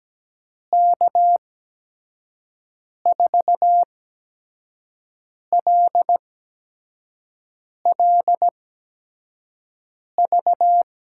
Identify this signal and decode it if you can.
Morse code, heard as a keyed tone, about 17 wpm, spaced as for 5 wpm, 700 Hz. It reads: K4LLV
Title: USA Call Signs - 8x Character Spacing - 17wpm